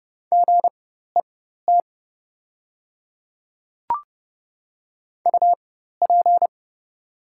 Morse code, heard as a keyed tone, about 30 wpm, spaced as for 14 wpm, 700 Hz. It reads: GET UP
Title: 4x Character Spacing - Top 200 Words - 30wpm